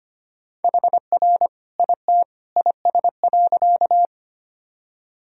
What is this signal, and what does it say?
Morse code, heard as a keyed tone, about 25 wpm, 700 Hz. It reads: HR IT IS.
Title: Sentences from Top 200 Words and Abbreviations 25wpm